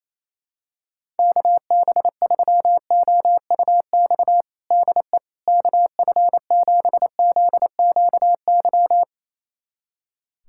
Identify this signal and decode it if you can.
Morse code, heard as a keyed tone, about 28 wpm, 700 Hz. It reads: KB3OUX DE KF7ZQY
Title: USA Callsign DE Callsign - 6 Characters - 28wpm